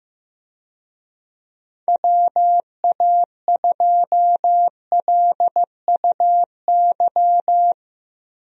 Morse code, heard as a keyed tone, about 15 wpm, 700 Hz. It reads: WA2LUY
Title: USA Call Signs - 6 Characters - 15wpm